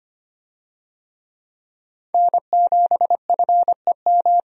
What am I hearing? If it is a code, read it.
Morse code, heard as a keyed tone, about 25 wpm, 700 Hz. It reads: N7FEM
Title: USA Call Signs - 5 Characters - 25wpm